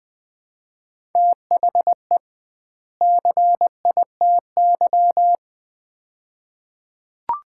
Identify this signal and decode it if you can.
Morse code, heard as a keyed tone, about 20 wpm, 700 Hz. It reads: THE CITY
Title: 2x Word-Spacing - Sets of 2 Words - 20wpm